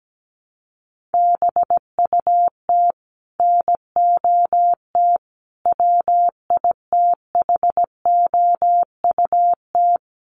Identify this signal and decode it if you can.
Morse code, heard as a keyed tone, about 17 wpm, 700 Hz. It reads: BUT NOT WITHOUT
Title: Sentences from Top 1500 Words 17wpm